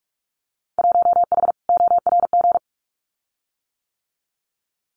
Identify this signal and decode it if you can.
Morse code, heard as a keyed tone, about 46 wpm, 700 Hz. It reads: 1H ORG
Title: Winter Field Day Exchanges 45wpm